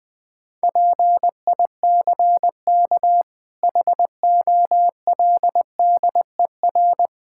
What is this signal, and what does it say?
Morse code, heard as a keyed tone, about 20 wpm, 700 Hz. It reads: PICK HOLDER
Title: Difficult Words - Speed-Racing with Hints - 20wpm